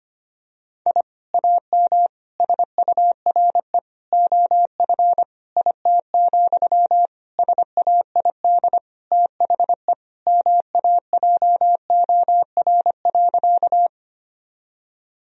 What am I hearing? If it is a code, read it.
Morse code, heard as a keyed tone, about 25 wpm, 700 Hz. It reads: I AM SURE OF IT, SAID THE MAJOR.
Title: Sentences from Top 1000 Words - Review Part 4 - 25wpm